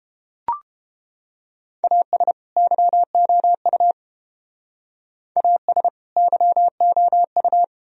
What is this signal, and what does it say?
Morse code, heard as a keyed tone, about 29 wpm, 700 Hz. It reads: AS YOU AS YOU